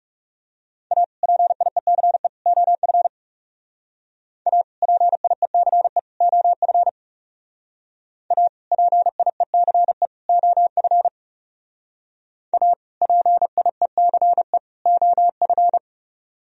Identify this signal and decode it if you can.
Morse code, heard as a keyed tone, about 38 wpm, 700 Hz. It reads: A PIECE OF A PIECE OF A PIECE OF A PIECE OF